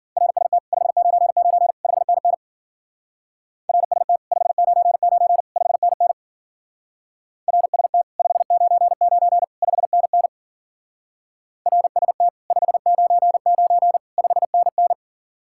Morse code, heard as a keyed tone, about 48 wpm, 700 Hz. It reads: RST 599 5NN RST 599 5NN RST 599 5NN RST 599 5NN
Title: Speed-Racing Common QSO Phrases Encore - 40wpm